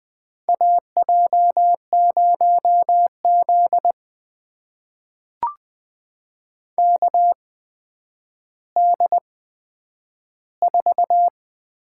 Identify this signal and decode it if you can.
Morse code, heard as a keyed tone, about 20 wpm, spaced as for 10 wpm, 700 Hz. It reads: AJ0Z KD4
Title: USA Call Signs - 8x Character Spacing - 20wpm